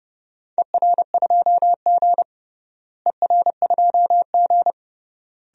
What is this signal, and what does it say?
Morse code, heard as a keyed tone, about 30 wpm, 700 Hz. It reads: ER2G ER2G
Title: DXCC-100 Call Signs - 4 Characters - Repeated 3x - 30wpm